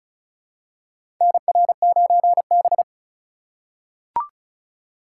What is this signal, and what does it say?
Morse code, heard as a keyed tone, about 36 wpm, 700 Hz. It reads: NR9B
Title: USA Call Signs - 4 Characters - 4x Character Spacing - 35wpm